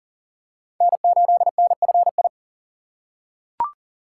Morse code, heard as a keyed tone, about 40 wpm, 700 Hz. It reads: N8NFI
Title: USA Call Signs - 5 Characters - 6x Character Spacing - 40wpm